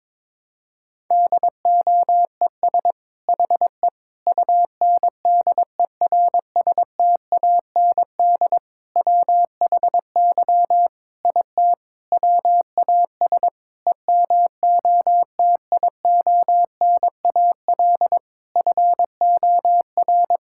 Morse code, heard as a keyed tone, about 22 wpm, 700 Hz. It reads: DOES HE UNDERSTAND WHY IT WAS EMOTIONAL FOR
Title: Sentences from Top 1000 Words - Review Part 5 - 22wpm